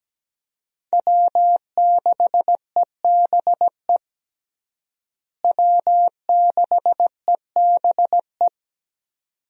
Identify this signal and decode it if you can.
Morse code, heard as a keyed tone, about 17 wpm, 700 Hz. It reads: W6EBE W6EBE
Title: USA Call Signs - 5 Characters - Repeated 2x - 17wpm